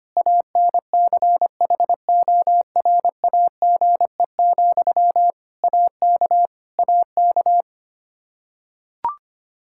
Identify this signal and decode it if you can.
Morse code, heard as a keyed tone, about 25 wpm, 700 Hz. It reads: ANCHORAGE, AK AK
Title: Practice QSO - Exchange 1 - Beginning - 25wpm